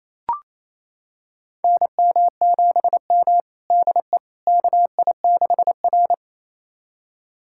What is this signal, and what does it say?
Morse code, heard as a keyed tone, about 28 wpm, 700 Hz. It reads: NM7M DE KI6R
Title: USA Callsign DE Callsign - 4 Characters - 28wpm